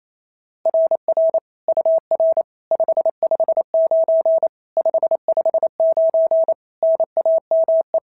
Morse code, heard as a keyed tone, about 28 wpm, 690 Hz. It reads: RR UR 559 559 NAME
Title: SOTA QSO Practice 28wpm